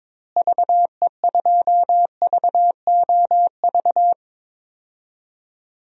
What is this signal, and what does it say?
Morse code, heard as a keyed tone, about 22 wpm, 700 Hz. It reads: VE2VOV
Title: DXCC-100 Call Signs - 6 Characters - Repeated 3x - 22wpm